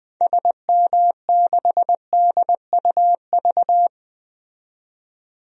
Morse code, heard as a keyed tone, about 20 wpm, 700 Hz. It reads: SM6DUV
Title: DXCC-100 Call Signs - 6 Characters - 20wpm